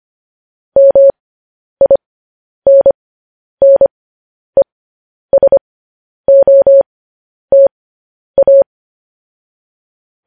Morse code, heard as a keyed tone, about 25 wpm, spaced as for 10 wpm, 550 Hz. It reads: MINNESOTA
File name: podcast / News headlines-1x-25x10